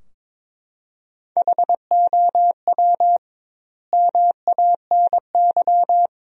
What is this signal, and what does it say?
Morse code, heard as a keyed tone, about 22 wpm, 700 Hz. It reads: HOW MANY
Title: 2x Word-Spacing - Sentences from Top 300 Words - 22wpm